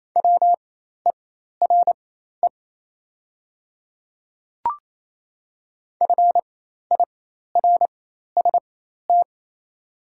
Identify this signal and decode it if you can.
Morse code, heard as a keyed tone, about 27 wpm, spaced as for 13 wpm, 690 Hz. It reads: WERE FIRST
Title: 4x Character Spacing - Top 500 Words - 28wpm